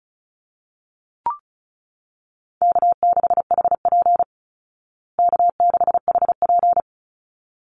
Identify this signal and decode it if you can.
Morse code, heard as a keyed tone, about 34 wpm, 700 Hz. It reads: K6HP K6HP
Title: Easy Call Signs Repeated 2x - 35wpm